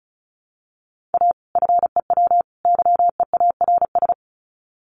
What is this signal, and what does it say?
Morse code, heard as a keyed tone, about 34 wpm, 700 Hz. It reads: A FEW YEARS
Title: Sets of 3 Words 35wpm